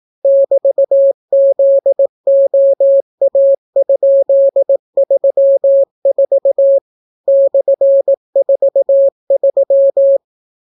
Morse code, heard as a keyed tone, about 18 wpm, 550 Hz. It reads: <BT>ZOA?34 /43